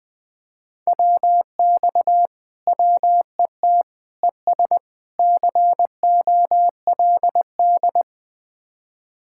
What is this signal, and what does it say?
Morse code, heard as a keyed tone, about 20 wpm, 700 Hz. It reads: WX WET ES COLD
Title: Speed-Racing Common QSO Phrases - 17wpm